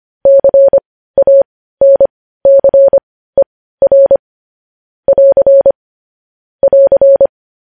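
Morse code, heard as a keyed tone, about 25 wpm, spaced as for 15 wpm, 550 Hz. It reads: CANCER <AR> <AR>
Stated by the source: Morse news headlines practice, possibly sent at 15 wpm.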